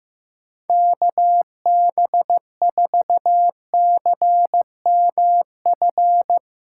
Morse code, heard as a keyed tone, about 15 wpm, 700 Hz. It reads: KB4CMF
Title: USA Callsign DE Callsign 15wpm